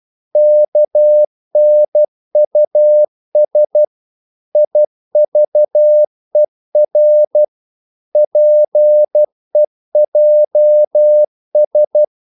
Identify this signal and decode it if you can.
Morse code, heard as a keyed tone, about 12 wpm, 600 Hz. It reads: KNUS IVER PEJS
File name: Korte ord DK 12wpm